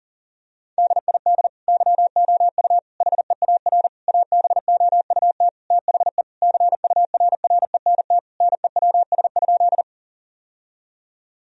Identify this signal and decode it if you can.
Morse code, heard as a keyed tone, about 40 wpm, 700 Hz. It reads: DID YOU HEAR ABOUT THE CURRENT NEWS?
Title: Sentences from Top 1000 Words - Review Part 4 - 40wpm